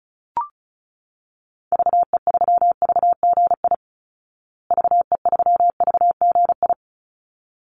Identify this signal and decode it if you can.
Morse code, heard as a keyed tone, about 34 wpm, 700 Hz. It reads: VE3VGI VE3VGI
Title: Call Signs Repeated 3x - 35wpm